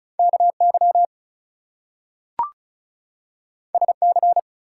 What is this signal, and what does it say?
Morse code, heard as a keyed tone, about 36 wpm, 700 Hz. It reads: KY SC
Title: US States and Canadian Provinces 35wpm